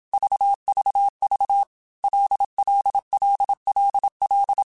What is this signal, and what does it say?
Morse code, heard as a keyed tone, about 27 wpm, 800 Hz. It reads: VVV LLLLL
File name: Basic Letter Groups By7Groups / LFCYQ / LFCYQ@25